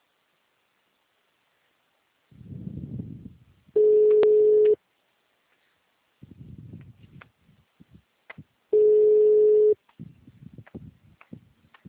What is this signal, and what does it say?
Ringback tone (Europe, 425 Hz).
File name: out- -2030- - - .44457